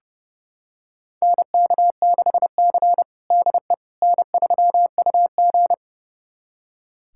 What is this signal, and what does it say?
Morse code, heard as a keyed tone, about 32 wpm, 700 Hz. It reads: NK6C DE N3UG
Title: USA Callsign DE Callsign - 4 Characters - 30wpm